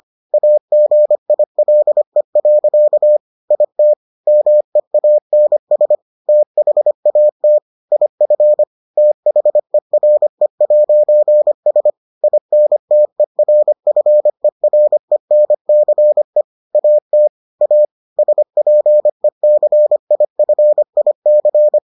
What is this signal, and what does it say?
Morse code, heard as a keyed tone, about 25 wpm, 600 Hz. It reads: AGILE. IT MEANS THAT IF THERE'S INTERFERENCE AT A SPECIFIC